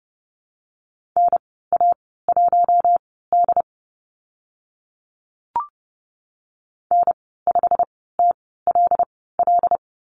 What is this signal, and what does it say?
Morse code, heard as a keyed tone, about 30 wpm, spaced as for 17 wpm, 700 Hz. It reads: NA1D N5TLL
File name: Lesson 15g - Number 5 - Call Signs 30wpm